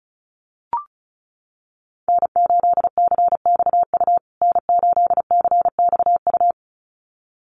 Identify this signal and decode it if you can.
Morse code, heard as a keyed tone, about 34 wpm, 700 Hz. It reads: N8CXU N8CXU
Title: USA Call Signs Repeated 2x 35wpm